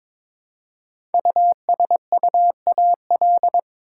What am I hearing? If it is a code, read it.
Morse code, heard as a keyed tone, about 22 wpm, 690 Hz. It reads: USUAL
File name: 5 Letter Words - First 3 Letters Given - 22wpm